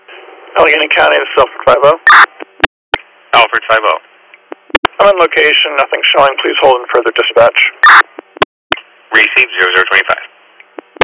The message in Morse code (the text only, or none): none